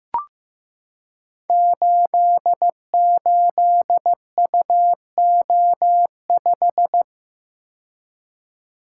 Morse code, heard as a keyed tone, about 15 wpm, 700 Hz. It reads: 88UO5
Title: Code Groups - 5 Letter:Numbers - 15wpm